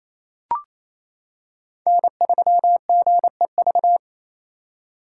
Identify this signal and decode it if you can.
Morse code, heard as a keyed tone, about 27 wpm, 700 Hz. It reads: N3GEV
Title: Code Groups - 5 Letter:Numbers - 28wpm